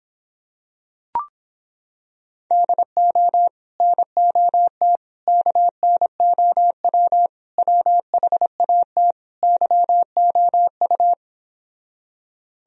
Morse code, heard as a keyed tone, about 26 wpm, 700 Hz. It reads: DO NOT KNOW WHAT YOU